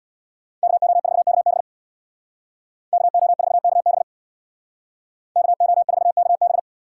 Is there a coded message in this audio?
Yes: Morse code from 0.6 to 6.6 s, heard as a keyed tone, about 64 wpm, 700 Hz. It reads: KO4KB KO4KB KO4KB